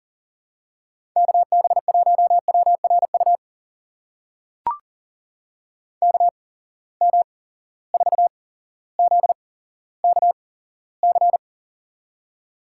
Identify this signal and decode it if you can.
Morse code, heard as a keyed tone, about 40 wpm, spaced as for 17 wpm, 700 Hz. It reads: KB1WRU KM4ZKC
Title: USA Call Signs - 6 Characters - 8x Character Spacing - 40wpm